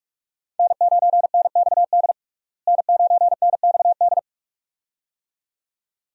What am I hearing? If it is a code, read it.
Morse code, heard as a keyed tone, about 46 wpm, 700 Hz. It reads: N9NXD N9NXD
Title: USA Call Signs - 5 Characters - Repeated 2x - 45wpm